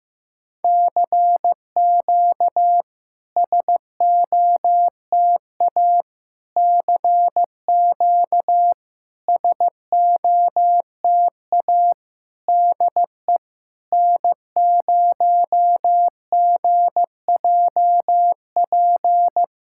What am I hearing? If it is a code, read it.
Morse code, heard as a keyed tone, about 15 wpm, 700 Hz. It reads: CQ SOTA CQ SOTA DE N0GJP
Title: SOTA QSO Practice 15wpm